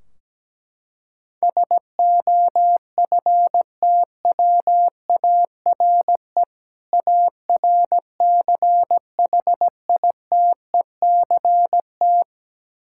Morse code, heard as a keyed tone, about 17 wpm, 700 Hz. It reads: SOFTWARE ARCHITECT
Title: Practice QSO - Exchange 3 - More Info - 17wpm